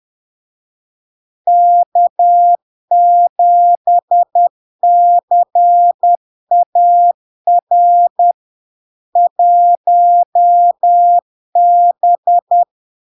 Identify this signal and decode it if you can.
Morse code, heard as a keyed tone, about 10 wpm, 700 Hz. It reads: K7CAR 1B
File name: LARK-FD-2022-Log-in-CW-10wpm-700-Hz